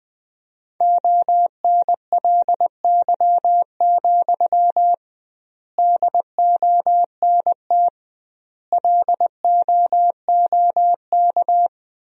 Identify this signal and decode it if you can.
Morse code, heard as a keyed tone, about 20 wpm, 700 Hz. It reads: ONLY, DONT LOOK
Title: 2x Word-Spacing - Sentences from Top 300 Words - 20wpm